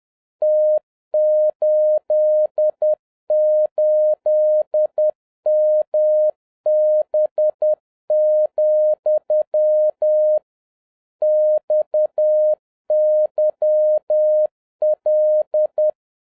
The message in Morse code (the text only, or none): T88MB, XYL